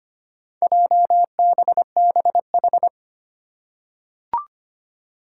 Morse code, heard as a keyed tone, about 25 wpm, 700 Hz. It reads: JBBH
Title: Four Challenging Letter-Number Combinations 25wpm